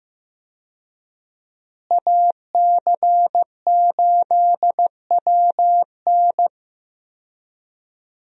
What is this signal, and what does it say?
Morse code, heard as a keyed tone, about 15 wpm, 700 Hz. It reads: AC8WN
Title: USA Call Signs - 5 Characters - Repeated 2x - 15wpm